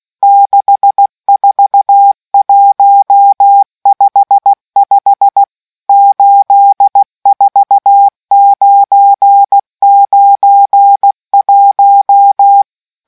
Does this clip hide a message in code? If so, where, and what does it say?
Yes: Morse code from 0.2 to 12.6 s, heard as a keyed tone, about 16 wpm, 800 Hz. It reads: 64155 84991